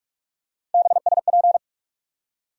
Morse code, heard as a keyed tone, about 44 wpm, 700 Hz. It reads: BSP
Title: USA Call Sign Postfixes - 3 Letters 45wpm